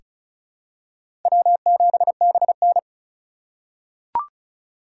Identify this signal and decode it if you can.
Morse code, heard as a keyed tone, about 34 wpm, 700 Hz. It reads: W7BN
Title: USA Call Signs - 4 Characters - 3x Character Spacing - 35wpm